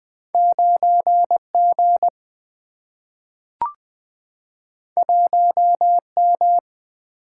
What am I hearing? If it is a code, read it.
Morse code, heard as a keyed tone, about 20 wpm, 700 Hz. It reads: 9G 1M
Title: Single-Letter Single-Number 20wpm